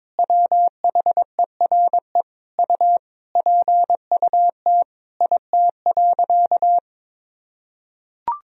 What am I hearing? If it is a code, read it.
Morse code, heard as a keyed tone, about 22 wpm, 700 Hz. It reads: WHERE U PUT IT.
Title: Sentences from Top 200 Words and Abbreviations 22wpm